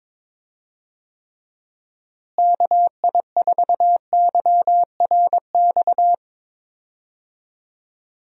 Morse code, heard as a keyed tone, about 22 wpm, 700 Hz. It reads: KI4YRX
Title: USA Call Signs - 6 Characters - Repeated 2x - 22wpm